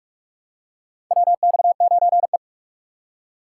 Morse code, heard as a keyed tone, about 47 wpm, 700 Hz. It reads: WX9E
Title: USA Call Signs - 4 Characters - 6x Character Spacing - 45wpm